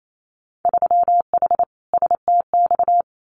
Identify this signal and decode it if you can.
Morse code, heard as a keyed tone, about 28 wpm, 700 Hz. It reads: 3H STX
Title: Winter Field Day Exchanges Encore 28wpm